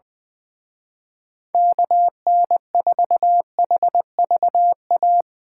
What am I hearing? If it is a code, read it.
Morse code, heard as a keyed tone, about 22 wpm, 700 Hz. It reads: KN4HVA
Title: Speed Racing - USA 2x3 Call Signs 20wpm